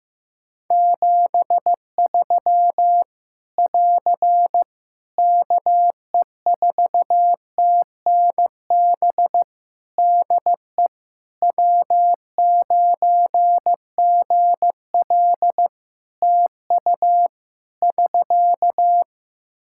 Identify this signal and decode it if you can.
Morse code, heard as a keyed tone, about 15 wpm, 700 Hz. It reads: 73 <AR> KE4TNB DE W9GL TU <SK>
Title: K1USN SST QSO Practice 15wpm